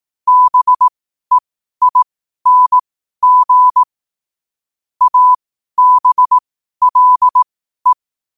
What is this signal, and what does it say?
Morse code, heard as a keyed tone, about 18 wpm, spaced as for 12 wpm, 1000 Hz. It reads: BEING ABLE
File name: qotdfiles / 12wpm